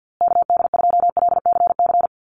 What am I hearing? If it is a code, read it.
Morse code, heard as a keyed tone, about 50 wpm, 700 Hz. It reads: KD2LCC
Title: USA Call Signs Repeated 2x 50wpm